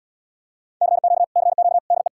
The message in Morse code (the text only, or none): CQ CQ DE